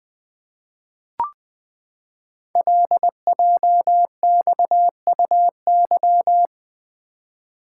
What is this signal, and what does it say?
Morse code, heard as a keyed tone, about 20 wpm, 700 Hz. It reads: LJXUY
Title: Code Groups - 5 Letter:Numbers - 20wpm